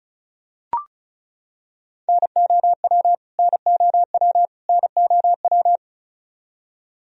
Morse code, heard as a keyed tone, about 34 wpm, 700 Hz. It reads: NOW NOW NOW
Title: Top 100 Words Repeated 35wpm